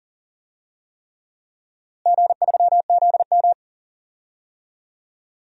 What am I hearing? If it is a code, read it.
Morse code, heard as a keyed tone, about 40 wpm, 700 Hz. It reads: G3ZM
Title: DXCC-100 Call Signs - 4 Characters - Repeated 3x - 40wpm